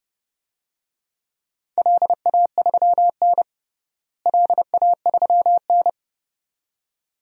Morse code, heard as a keyed tone, about 30 wpm, 700 Hz. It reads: LA3N LA3N
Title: DXCC-100 Call Signs - 4 Characters - Repeated 2x - 30wpm